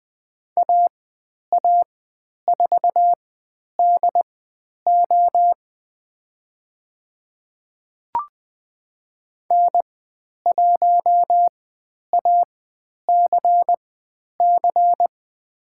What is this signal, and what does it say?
Morse code, heard as a keyed tone, about 20 wpm, spaced as for 10 wpm, 700 Hz. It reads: AA4DO N1ACC
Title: 102 Lesson 17g - Letter u - Call Signs 20wpm